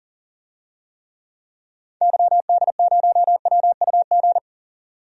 Morse code, heard as a keyed tone, about 37 wpm, 700 Hz. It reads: YD0WUG
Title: DXCC-100 Call Signs - 6 Characters - 40wpm